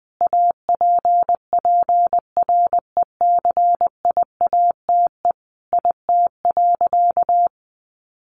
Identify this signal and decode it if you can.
Morse code, heard as a keyed tone, about 20 wpm, 700 Hz. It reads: APPRECIATE IT.
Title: Sentences from Top 1300 Words 20wpm